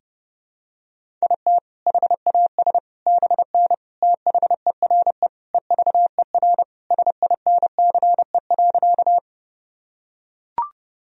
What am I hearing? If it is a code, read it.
Morse code, heard as a keyed tone, about 30 wpm, 700 Hz. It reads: IT HAS BN THERE EVER SINCE.
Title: Sentences from Top 400 Words and Abbreviations 30wpm